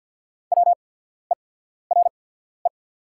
Morse code, heard as a keyed tone, about 48 wpm, spaced as for 14 wpm, 700 Hz. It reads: WERE
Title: 8x Character Spacing - Top 100 Words - 50wpm